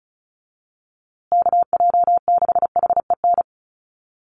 Morse code, heard as a keyed tone, about 34 wpm, 700 Hz. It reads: KJ6HEN
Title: USA Call Signs Repeated 2x 35wpm